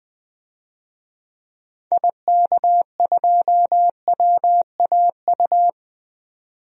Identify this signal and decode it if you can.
Morse code, heard as a keyed tone, about 20 wpm, 700 Hz. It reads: IK2WAU
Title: DXCC-100 Call Signs - 6 Characters - 20wpm